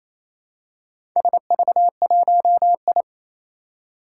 Morse code, heard as a keyed tone, about 28 wpm, 700 Hz. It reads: SV1I
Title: Four Challenging Letter-Number Combinations 28wpm